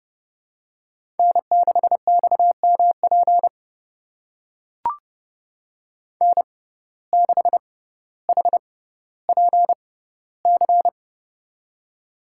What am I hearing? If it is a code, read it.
Morse code, heard as a keyed tone, about 29 wpm, spaced as for 15 wpm, 700 Hz. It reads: N6XMP N6HPC
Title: USA Call Signs - 5 Characters - 6x Character Spacing - 30wpm